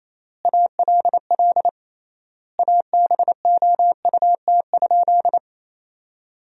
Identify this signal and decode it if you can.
Morse code, heard as a keyed tone, about 28 wpm, 700 Hz. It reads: ALL ABOUT?
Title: 3x Word-Spacing - Sentences from Top 100 Words - 28wpm